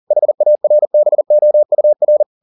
Morse code, heard as a keyed tone, about 40 wpm, 600 Hz. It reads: HARBOUR